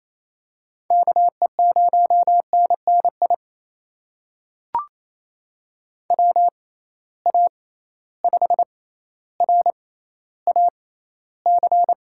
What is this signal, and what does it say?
Morse code, heard as a keyed tone, about 28 wpm, spaced as for 14 wpm, 700 Hz. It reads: KE0NNI WA5RAC
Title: USA Call Signs - 6 Characters - 6x Character Spacing - 28wpm